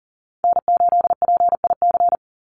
Morse code, heard as a keyed tone, about 40 wpm, 700 Hz. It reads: N8PIC
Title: USA Call Signs 40wpm